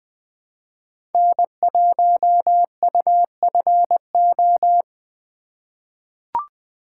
Morse code, heard as a keyed tone, about 20 wpm, 700 Hz. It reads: N1UFO
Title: USA Call Signs - 5 Characters - 3x Character Spacing - 20wpm